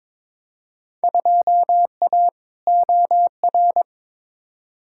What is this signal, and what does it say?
Morse code, heard as a keyed tone, about 22 wpm, 700 Hz. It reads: 2A OR
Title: Field Day Exchanges - 22wpm